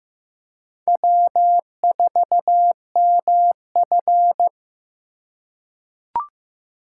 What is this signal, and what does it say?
Morse code, heard as a keyed tone, about 16 wpm, 700 Hz. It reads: W4MF
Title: USA Call Signs - 4 Characters - 4x Character Spacing - 15wpm